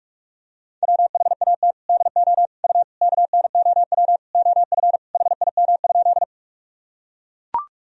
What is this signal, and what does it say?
Morse code, heard as a keyed tone, about 42 wpm, 700 Hz. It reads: WHAT DO U KNOW OF HIM?
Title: Sentences from Top 100 Words and Abbreviations 45wpm